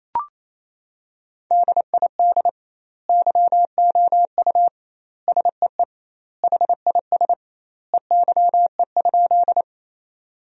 Morse code, heard as a keyed tone, about 28 wpm, 700 Hz. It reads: DID YOU SEE HIS EYE?
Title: 2x Word-Spacing - Sentences from Top 300 Words - 28wpm